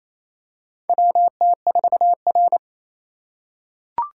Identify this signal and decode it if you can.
Morse code, heard as a keyed tone, about 27 wpm, 700 Hz. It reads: WT4R
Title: USA Call Signs - 4 Characters - 4x Character Spacing - 28wpm